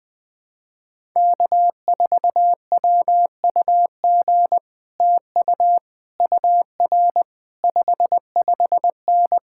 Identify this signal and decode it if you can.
Morse code, heard as a keyed tone, about 20 wpm, 700 Hz. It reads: K4WUG TU UR 55N
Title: Speed-Racing Common QSO Phrases - 20wpm